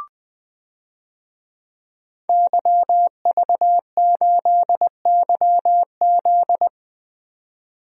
Morse code, heard as a keyed tone, about 20 wpm, 700 Hz. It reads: YV8YZ
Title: DXCC-100 Call Signs - 5 Characters - 20wpm